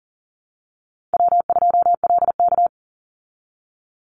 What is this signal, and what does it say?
Morse code, heard as a keyed tone, about 40 wpm, 700 Hz. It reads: W2LK
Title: Easy Call Signs Repeated 3x - 40wpm